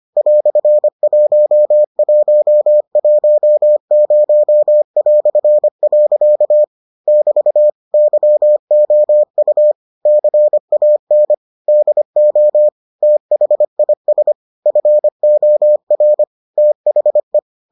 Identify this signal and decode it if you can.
Morse code, heard as a keyed tone, about 25 wpm, 600 Hz. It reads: "1110". <BT> YOU CAN DO THIS FOR THE